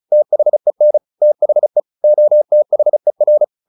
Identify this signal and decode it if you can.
Morse code, heard as a keyed tone, about 34 wpm, 600 Hz. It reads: THEN THE OTHER
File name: the old man and the sea 2 35wpm